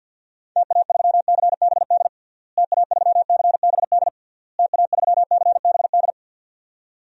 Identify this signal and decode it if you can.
Morse code, heard as a keyed tone, about 51 wpm, 700 Hz. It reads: TA3CBD TA3CBD TA3CBD
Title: DXCC-100 Call Signs - 6 Characters - Repeated 3x - 50wpm